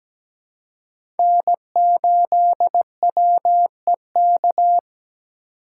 Morse code, heard as a keyed tone, about 17 wpm, 700 Hz. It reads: N8WEK
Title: USA Call Signs - 5 Characters - 2x Character Spacing - 17wpm